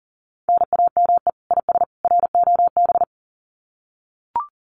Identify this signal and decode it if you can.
Morse code, heard as a keyed tone, about 40 wpm, 700 Hz. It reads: NAME IS ROB
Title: Common QSO Phrases 40wpm